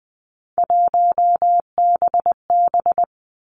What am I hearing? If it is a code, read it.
Morse code, heard as a keyed tone, about 20 wpm, 700 Hz. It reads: 1BB
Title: Three Challenging Letter-Number Combinations 20wpm